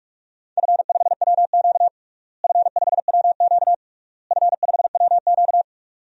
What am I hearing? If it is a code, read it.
Morse code, heard as a keyed tone, about 41 wpm, 690 Hz. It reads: F5WQ F5WQ F5WQ
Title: DXCC-100 Call Signs - 4 Characters - Repeated 3x - 45wpm